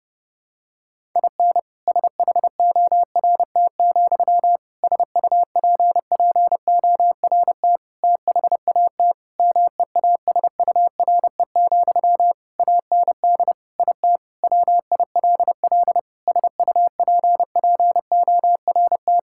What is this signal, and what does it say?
Morse code, heard as a keyed tone, about 30 wpm, 700 Hz. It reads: IN SHORT, SUPPORT THAT MEASURE, AND IT WILL SUPPORT
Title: Sentences from Top 1000 Words - Review Part 5 - 30wpm